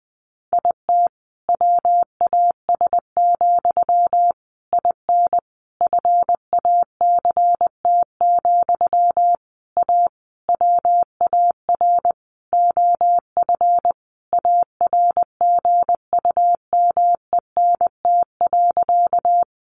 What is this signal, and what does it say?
Morse code, heard as a keyed tone, about 20 wpm, 700 Hz. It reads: IT WAS, IN FACT, A WAR OF ARGUMENT.
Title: Sentences from Top 1400 Words 20wpm